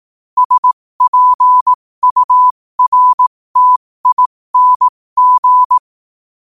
Morse code, heard as a keyed tone, about 18 wpm, spaced as for 15 wpm, 1000 Hz. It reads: SPURTING
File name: qotdfiles / 15wpm